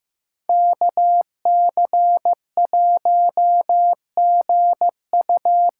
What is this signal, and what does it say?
Morse code, heard as a keyed tone, about 15 wpm, 700 Hz. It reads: KC1GU
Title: USA Call Signs - 6 Characters - 6x Character Spacing - 15wpm